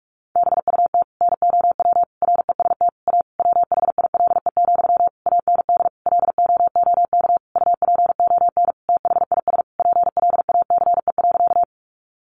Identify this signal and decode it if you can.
Morse code, heard as a keyed tone, about 44 wpm, 700 Hz. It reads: BUT NOW REST A WHILE, AND LOOK UPON THIS PLACE.
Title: Sentences from Top 1100 Words 45wpm